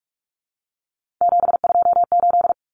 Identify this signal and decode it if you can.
Morse code, heard as a keyed tone, about 44 wpm, 700 Hz. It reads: 728
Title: Common Cut Numbers - 3-Digit 45wpm